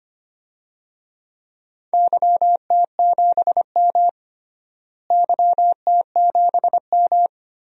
Morse code, heard as a keyed tone, about 25 wpm, 700 Hz. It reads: YT7M YT7M
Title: DXCC-100 Call Signs - 4 Characters - Repeated 3x - 25wpm